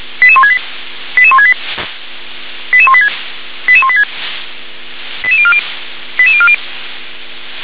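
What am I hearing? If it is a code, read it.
Five-tone selective call (ZVEI), heard as quick runs of five tones, tones of about 70 ms. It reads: 88168 88168 88186 88186 88040 88040
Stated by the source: ZVEI five-tone selective call 88058, received by SDR.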